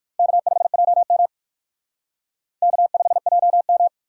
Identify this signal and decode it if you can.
Morse code, heard as a keyed tone, about 56 wpm, 700 Hz. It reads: K5JM K5JM
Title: Speed Racing - USA 1x2 Call Signs 45wpm